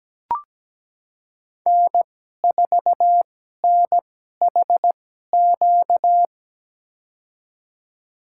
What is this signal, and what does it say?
Morse code, heard as a keyed tone, about 17 wpm, spaced as for 12 wpm, 700 Hz. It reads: N4NHQ
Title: USA Call Signs - 2x Character Spacing - 17wpm